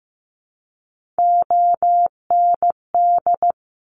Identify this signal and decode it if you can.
Morse code, heard as a keyed tone, about 15 wpm, 700 Hz. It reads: OND
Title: Top Three Letter Combinations in Words 15wpm